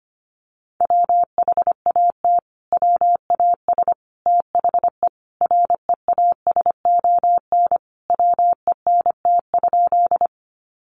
Morse code, heard as a keyed tone, about 25 wpm, 700 Hz. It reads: WHAT WAS THE REASON WENT?
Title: Sentences from Top 700 Words 25wpm